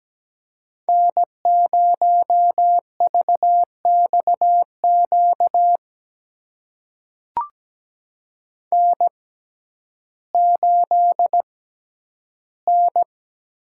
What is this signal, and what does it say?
Morse code, heard as a keyed tone, about 17 wpm, spaced as for 10 wpm, 700 Hz. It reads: N0VXQ N8N
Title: USA Call Signs - 5 Characters - 6x Character Spacing - 17wpm